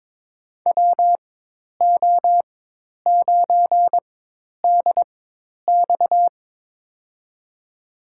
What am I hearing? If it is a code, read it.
Morse code, heard as a keyed tone, about 22 wpm, spaced as for 10 wpm, 700 Hz. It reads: WO9DX
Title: USA Call Signs - 5 Characters - 4x Character Spacing - 22wpm